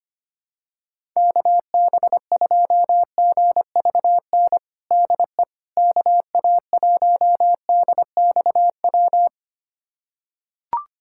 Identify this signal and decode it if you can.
Morse code, heard as a keyed tone, about 25 wpm, 700 Hz. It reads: KB2GVN DE KA1DXW
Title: USA Callsign DE Callsign - 6 Characters - 25wpm